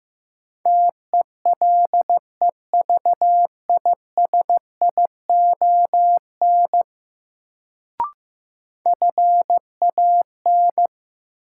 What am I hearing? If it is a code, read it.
Morse code, heard as a keyed tone, about 15 wpm, 700 Hz. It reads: TELEVISION FAN
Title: Sentences from Top 1000 Words - Review Part 4 - New Word Drill - 15wpm